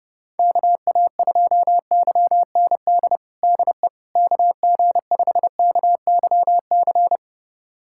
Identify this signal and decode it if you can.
Morse code, heard as a keyed tone, about 32 wpm, 700 Hz. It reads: KA2YND DE KG5KYC
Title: USA Callsign DE Callsign - 6 Characters - 30wpm